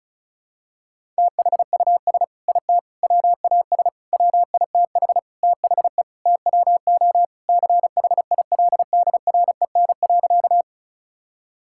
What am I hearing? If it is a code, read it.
Morse code, heard as a keyed tone, about 34 wpm, 700 Hz. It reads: THUS IT WAS WITH THE TWO CHILDREN.
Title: Sentences from Top 1000 Words - Review Part 5 - 35wpm